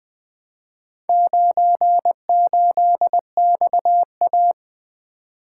Morse code, heard as a keyed tone, about 20 wpm, 700 Hz. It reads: 98XA
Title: Four Letter-Number Combinations - 20wpm